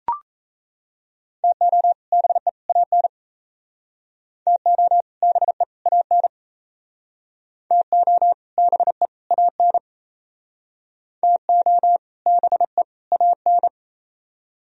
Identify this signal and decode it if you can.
Morse code, heard as a keyed tone, about 34 wpm, 700 Hz. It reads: TO BE AN TO BE AN TO BE AN TO BE AN
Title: Speed-Racing - Sets of 3 Words 28wpm